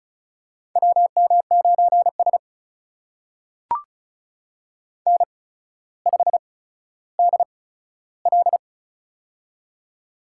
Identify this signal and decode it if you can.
Morse code, heard as a keyed tone, about 36 wpm, spaced as for 15 wpm, 700 Hz. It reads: WM9S N5DL
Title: USA Call Signs - 4 Characters - 8x Character Spacing - 35wpm